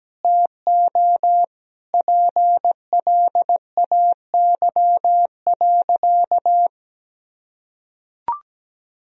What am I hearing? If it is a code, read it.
Morse code, heard as a keyed tone, about 17 wpm, 700 Hz. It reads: TO PLAY.
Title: Sentences from Top 1000 Words - Review Part 1 - 17wpm